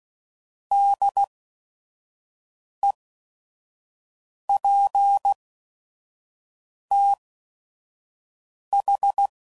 Morse code, heard as a keyed tone, about 16 wpm, spaced as for 5 wpm, 800 Hz. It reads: DEPTH
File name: Prefixes / By First Letter / Ds